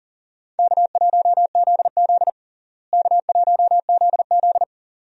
Morse code, heard as a keyed tone, about 37 wpm, 700 Hz. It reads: K1ZZ K1ZZ
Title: USA Call Signs - 4 Characters - Repeated 2x - 40wpm